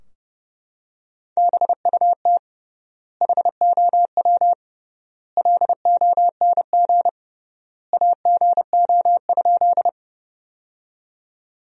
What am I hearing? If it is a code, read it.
Morse code, heard as a keyed tone, about 32 wpm, 700 Hz. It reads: BUT HOW LONG AGO?
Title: 3x Word-Spacing - Sentences from Top 500 Words - 30wpm